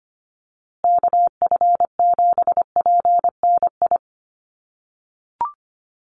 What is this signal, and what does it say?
Morse code, heard as a keyed tone, about 25 wpm, 700 Hz. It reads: KF7PNI
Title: USA Call Signs Repeated 3x 25wpm